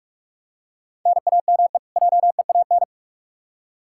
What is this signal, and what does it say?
Morse code, heard as a keyed tone, about 46 wpm, 700 Hz. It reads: NAME JEAN
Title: Practice QSO - Exchange 1 - Beginning - 45wpm